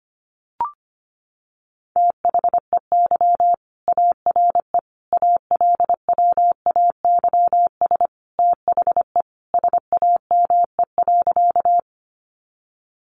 Morse code, heard as a keyed tone, about 25 wpm, 700 Hz. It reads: THEY ARE ALWAYS THE SAME.
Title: Sentences from Top 300 Words 25wpm